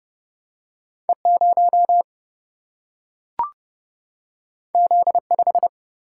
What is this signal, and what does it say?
Morse code, heard as a keyed tone, about 30 wpm, 700 Hz. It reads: E0 Z5
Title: Single-Letter Single-Number 30wpm